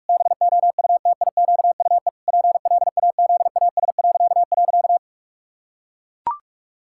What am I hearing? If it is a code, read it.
Morse code, heard as a keyed tone, about 41 wpm, 700 Hz. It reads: BOUTIQUE PLAZAS..
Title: Pangram Challenge v2 - 45wpm